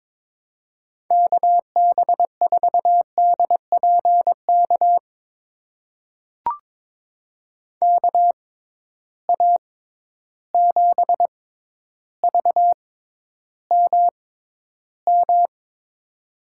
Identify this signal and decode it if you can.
Morse code, heard as a keyed tone, about 22 wpm, spaced as for 11 wpm, 700 Hz. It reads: KB4DPK KA7VMM
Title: USA Call Signs - 6x Character Spacing - 22wpm